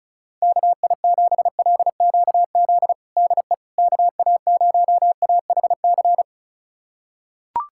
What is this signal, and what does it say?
Morse code, heard as a keyed tone, about 36 wpm, 700 Hz. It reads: KI7LQZ DE KA0AHC
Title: USA Callsign DE Callsign 35wpm